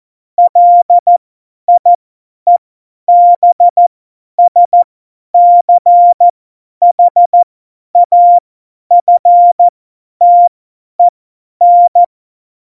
Morse code, heard as a keyed tone, about 14 wpm, spaced as for 10 wpm, 700 Hz. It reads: LIEBSCHAFTEN
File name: Morsekurs / Witze / Witz 01. Dezember 2020, 50 F70